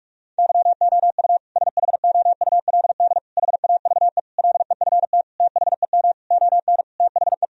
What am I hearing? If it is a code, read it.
Morse code, heard as a keyed tone, about 41 wpm, 700 Hz. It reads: YOU SHOULD HAVE LEFT THEM ON THE
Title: Sentences from Top 1000 Words - Review Part 1 - 45wpm